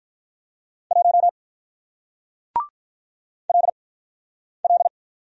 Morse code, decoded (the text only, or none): 1 LL